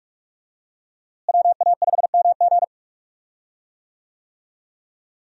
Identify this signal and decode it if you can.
Morse code, heard as a keyed tone, about 43 wpm, 700 Hz. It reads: WA5MG
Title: USA Call Signs - 5 Characters - Repeated 2x - 45wpm